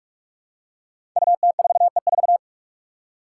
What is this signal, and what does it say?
Morse code, heard as a keyed tone, about 46 wpm, 700 Hz. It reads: UT4E4
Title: Five Challenging Letter-Number Combinations 45wpm